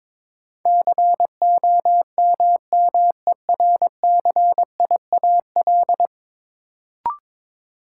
Morse code, heard as a keyed tone, about 22 wpm, 700 Hz. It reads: COMMERCIAL
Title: Sentences from Top 1000 Words - Review Part 5 - New Word Drill - 22wpm